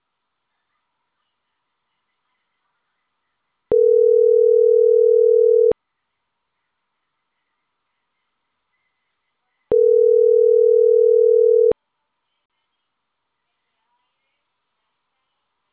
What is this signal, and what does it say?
Ringback tone (North America), 440 + 480 Hz.
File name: CENTRAL CAEA / out agcaea01 - .73571